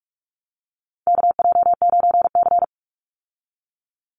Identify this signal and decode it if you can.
Morse code, heard as a keyed tone, about 44 wpm, 700 Hz. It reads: KJ9C
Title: Call Signs - 45wpm